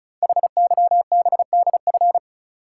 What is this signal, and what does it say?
Morse code, heard as a keyed tone, about 36 wpm, 690 Hz. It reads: HYBDF
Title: Code Groups - 5 Letter:Numbers - 35wpm